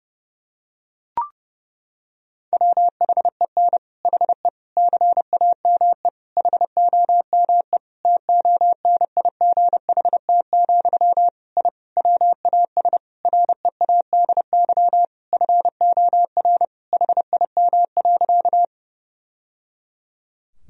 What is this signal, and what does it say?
Morse code, heard as a keyed tone, about 30 wpm, 700 Hz. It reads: WHEN HE CAME HOME TONIGHT, I WAS READY FOR HIM.
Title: Sentences from Top 1000 Words - Review Part 3 - 30wpm